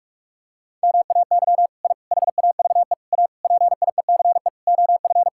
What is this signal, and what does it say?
Morse code, heard as a keyed tone, about 46 wpm, 700 Hz. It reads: MAY I HAVE A PIECE OF
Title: Sentences from Top 1000 Words - Review Part 5 - 45wpm